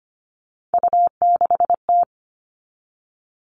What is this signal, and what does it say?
Morse code, heard as a keyed tone, about 25 wpm, 700 Hz. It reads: U6T
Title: All Cut Numbers - 3-Digit 25wpm